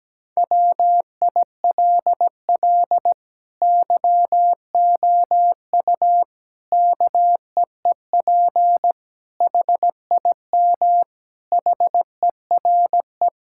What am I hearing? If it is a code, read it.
Morse code, heard as a keyed tone, about 17 wpm, 700 Hz. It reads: WILL YOU KEEP HIM HERE
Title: Sentences from Top 1600 Words 17wpm